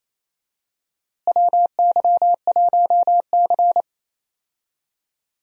Morse code, heard as a keyed tone, about 27 wpm, 700 Hz. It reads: WY1C
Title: USA Call Signs - 4 Characters - Repeated 3x - 28wpm